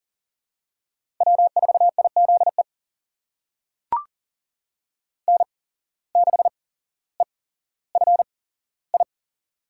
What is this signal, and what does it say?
Morse code, heard as a keyed tone, about 40 wpm, spaced as for 17 wpm, 700 Hz. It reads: W4IZE N6EFI
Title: USA Call Signs - 5 Characters - 8x Character Spacing - 40wpm